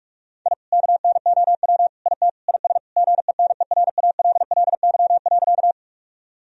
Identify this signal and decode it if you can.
Morse code, heard as a keyed tone, about 46 wpm, 700 Hz. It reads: I KNOW IT IS GENERALLY.
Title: Sentences from Top 1000 Words - Review Part 4 - 45wpm